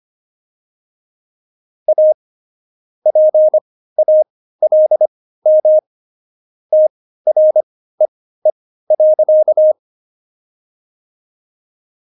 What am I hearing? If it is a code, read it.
Morse code, heard as a keyed tone, about 25 wpm, spaced as for 15 wpm, 620 Hz. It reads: A PALM TREE.
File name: Jokes-25-char-15-eff